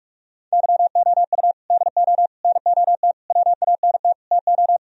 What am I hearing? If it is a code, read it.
Morse code, heard as a keyed tone, about 44 wpm, 700 Hz. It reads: YOU DO NOT WANT TO